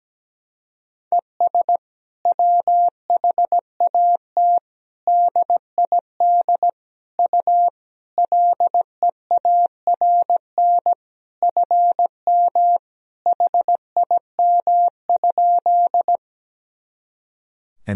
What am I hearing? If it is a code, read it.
Morse code, heard as a keyed tone, about 17 wpm, 700 Hz. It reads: ES WHAT DID U LEARN FM HIM?
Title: Sentences from Top 300 Words and Abbreviations 17wpm